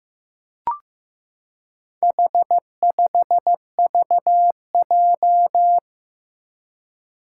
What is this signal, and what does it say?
Morse code, heard as a keyed tone, about 15 wpm, 700 Hz. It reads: H5VJ
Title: Four Challenging Letter-Number Combinations 15wpm